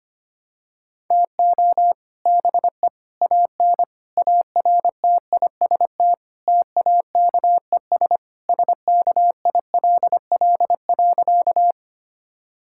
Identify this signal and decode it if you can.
Morse code, heard as a keyed tone, about 25 wpm, 700 Hz. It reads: TO BE AN ARTIST TAKES SKILL.
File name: practice / 25wpm / Sentences with 3-Word Phrases - 25wpm